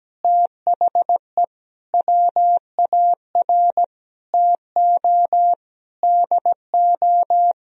Morse code, heard as a keyed tone, about 17 wpm, 700 Hz. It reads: THE WAR TO DO
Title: Sentences from Top 1000 Words - Review Part 2 - 17wpm